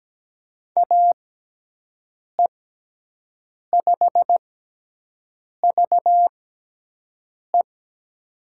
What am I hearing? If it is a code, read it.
Morse code, heard as a keyed tone, about 17 wpm, 700 Hz. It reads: A E 5 V E
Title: USA Call Signs - 5 Characters - 6x Character Spacing - 17wpm